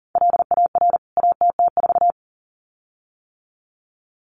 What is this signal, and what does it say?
Morse code, heard as a keyed tone, about 40 wpm, 700 Hz. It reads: LAR ATT4
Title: CWT Exchange 40wpm